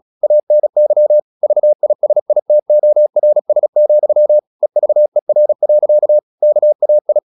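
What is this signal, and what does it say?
Morse code, heard as a keyed tone, about 36 wpm, 600 Hz. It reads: ANY VISITORS, EVER. KAI
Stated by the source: Morse practice at 36 wpm, text of Ken Follett's novel Never, from CW med Gnister.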